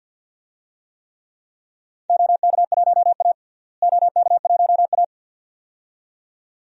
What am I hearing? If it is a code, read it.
Morse code, heard as a keyed tone, about 47 wpm, 700 Hz. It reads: OK1A OK1A
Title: DXCC-100 Call Signs - 4 Characters - Repeated 2x - 50wpm